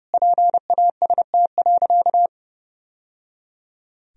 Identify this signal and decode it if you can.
Morse code, heard as a keyed tone, about 30 wpm, 700 Hz. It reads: PAST.
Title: Sentences from Top 1000 Words - Review Part 3 - 30wpm